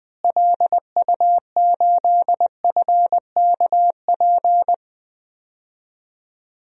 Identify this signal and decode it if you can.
Morse code, heard as a keyed tone, about 20 wpm, 700 Hz. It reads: LU8FKP
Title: DXCC-100 Call Signs - 6 Characters - Repeated 3x - 20wpm